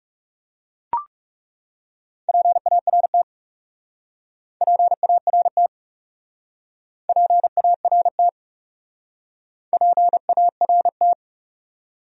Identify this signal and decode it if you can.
Morse code, heard as a keyed tone, about 40 wpm, 700 Hz. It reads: PART PART PART PART